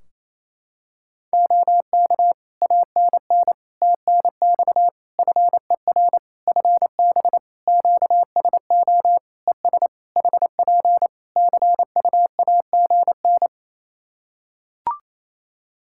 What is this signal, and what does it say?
Morse code, heard as a keyed tone, about 27 wpm, 700 Hz. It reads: OK ANN TNX FER FB QSO ES HP CUAGN
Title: Practice QSO - Exchange 4 - Ending - 28wpm